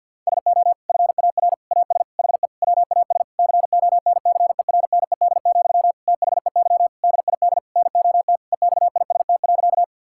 Morse code, heard as a keyed tone, about 52 wpm, 700 Hz. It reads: SO FAR AS HE WAS CONCERNED, THEY DID NOT EXIST.
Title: Sentences from Top 1000 Words - Review Part 6 - 50wpm